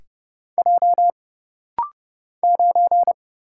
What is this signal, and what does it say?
Morse code, heard as a keyed tone, about 30 wpm, 700 Hz. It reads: J 9
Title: Single Letter-Number Alt - Rapid-Fire 30wpm